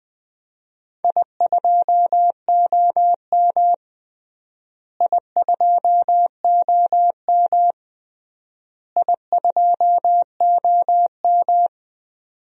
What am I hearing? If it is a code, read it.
Morse code, heard as a keyed tone, about 20 wpm, 700 Hz. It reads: I2OM I2OM I2OM
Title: DXCC-100 Call Signs - 4 Characters - Repeated 3x - 20wpm